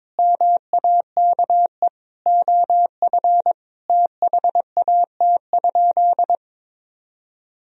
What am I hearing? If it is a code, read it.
Morse code, heard as a keyed tone, about 22 wpm, 700 Hz. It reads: MAKE OF THAT?
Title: Sentences from Top 100 Words and Abbreviations 22wpm